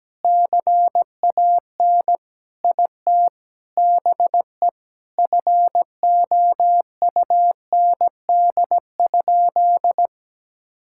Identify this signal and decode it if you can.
Morse code, heard as a keyed tone, about 17 wpm, 700 Hz. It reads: CAN IT BE FOUND?
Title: Sentences from Top 1600 Words 17wpm